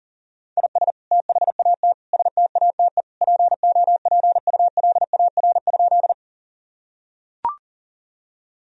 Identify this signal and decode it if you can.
Morse code, heard as a keyed tone, about 40 wpm, 700 Hz. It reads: IS THAT STATE POPULAR?
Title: Sentences from Top 1000 Words - Review Part 3 - 40wpm